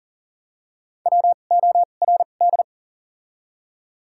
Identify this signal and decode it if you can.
Morse code, heard as a keyed tone, about 40 wpm, spaced as for 29 wpm, 700 Hz. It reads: WORD
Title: 2x Character Spacing - Top 200 Words - 40wpm